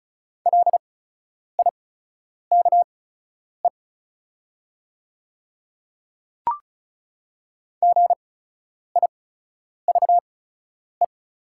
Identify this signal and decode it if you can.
Morse code, heard as a keyed tone, about 36 wpm, spaced as for 10 wpm, 700 Hz. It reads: LIKE GIVE
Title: 8x Character Spacing - Top 500 Words - 35wpm